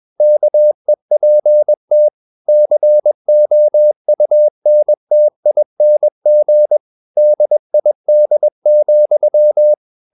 Morse code, heard as a keyed tone, about 21 wpm, 600 Hz. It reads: KEPT COUNTING DID,
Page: CW med Gnister